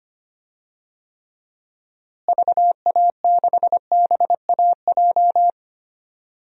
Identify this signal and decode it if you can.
Morse code, heard as a keyed tone, about 25 wpm, 700 Hz. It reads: VA6BAJ
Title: DXCC-100 Call Signs - 6 Characters - Repeated 3x - 25wpm